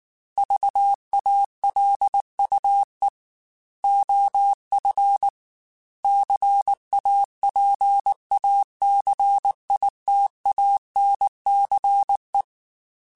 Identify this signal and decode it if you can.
Morse code, heard as a keyed tone, about 19 wpm, 800 Hz. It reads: VALUE OF CAPACITANCE